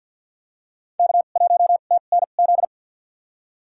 Morse code, heard as a keyed tone, about 52 wpm, spaced as for 36 wpm, 700 Hz. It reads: K1TNZ
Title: USA Call Signs - 5 Characters - 2x Character Spacing - 50wpm